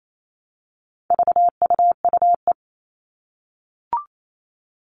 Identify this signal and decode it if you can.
Morse code, heard as a keyed tone, about 28 wpm, 700 Hz. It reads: VUUE
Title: All Cut Numbers - 4-Digit 28wpm